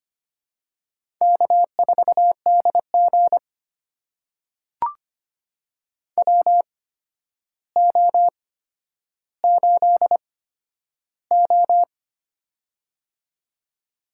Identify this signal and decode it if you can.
Morse code, heard as a keyed tone, about 25 wpm, spaced as for 11 wpm, 700 Hz. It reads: K4DG WO8O
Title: USA Call Signs - 4 Characters - 8x Character Spacing - 25wpm